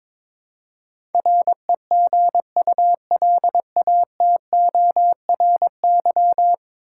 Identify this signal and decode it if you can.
Morse code, heard as a keyed tone, about 21 wpm, 700 Hz. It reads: REGULATORY
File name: practice / 22wpm / Ten Letter Words - 22wpm